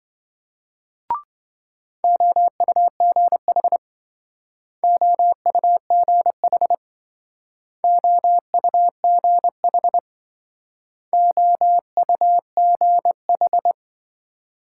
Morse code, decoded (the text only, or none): OUGH OUGH OUGH OUGH